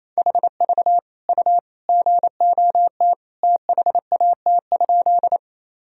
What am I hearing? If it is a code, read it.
Morse code, heard as a keyed tone, about 28 wpm, 700 Hz. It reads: HV U GOT THAT?
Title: Sentences from Top 300 Words and Abbreviations 28wpm